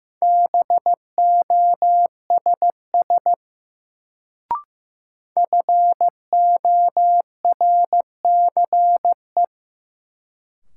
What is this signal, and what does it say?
Morse code, heard as a keyed tone, about 15 wpm, 700 Hz. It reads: BOSS FORCE
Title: Sentences from Top 700 Words - New Word Drill - 15wpm